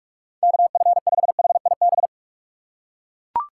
Morse code, heard as a keyed tone, about 41 wpm, 700 Hz. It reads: KF5HIB
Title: USA Call Signs - 2x Character Spacing - 45wpm